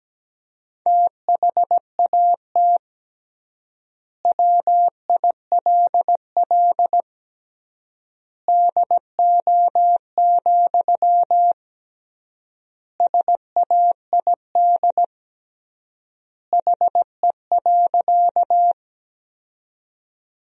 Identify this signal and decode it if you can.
Morse code, heard as a keyed tone, about 17 wpm, 700 Hz. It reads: THAT WILL DO, SAID HE.
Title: 3x Word-Spacing - Sentences from Top 100 Words - 17wpm